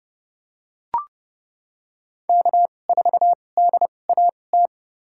Morse code, heard as a keyed tone, about 30 wpm, spaced as for 22 wpm, 700 Hz. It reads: K4DAT
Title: USA Call Signs - 2x Character Spacing - 30wpm